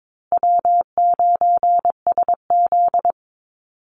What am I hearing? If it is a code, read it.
Morse code, heard as a keyed tone, about 22 wpm, 700 Hz. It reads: W9SZ
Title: Call Signs Repeated 3x - 22wpm